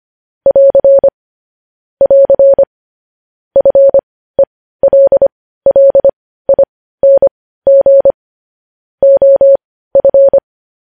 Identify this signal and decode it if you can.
Morse code, heard as a keyed tone, about 25 wpm, spaced as for 15 wpm, 550 Hz. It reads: <AR> <AR> FELLING OF
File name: podcast / News headlines-1x-25x15